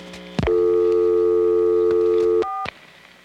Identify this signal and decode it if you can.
Telephone call progress tone (North America), heard as a telephone network's tone, about 350, 440 Hz. Dial tone.